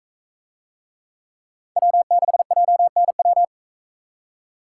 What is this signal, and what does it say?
Morse code, heard as a keyed tone, about 42 wpm, 700 Hz. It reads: W6JNW